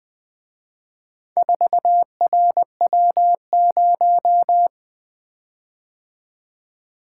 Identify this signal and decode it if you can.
Morse code, heard as a keyed tone, about 24 wpm, 700 Hz. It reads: 4RW0
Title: Speed-Racing - Four Letter-Number Combinations - 20wpm